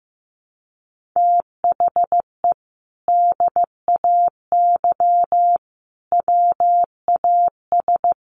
Morse code, heard as a keyed tone, about 15 wpm, 700 Hz. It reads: THE DAY WAS
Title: Sentences from Top 1500 Words 15wpm